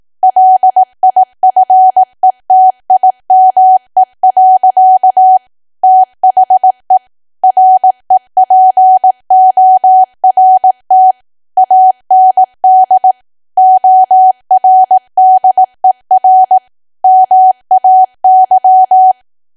Morse code, heard as a keyed tone, about 18 wpm, 750 Hz. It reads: LIFETIME. THE REPORT AND ORDER MAY
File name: cwa / W1AW-Level2 / 18WPM